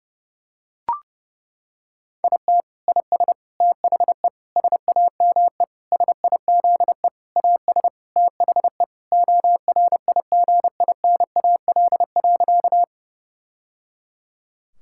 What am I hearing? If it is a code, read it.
Morse code, heard as a keyed tone, about 32 wpm, 700 Hz. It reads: IT IS THE SAME SIZE AS THE ORIGINAL.
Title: Sentences from Top 1000 Words - Review Part 6 - 30wpm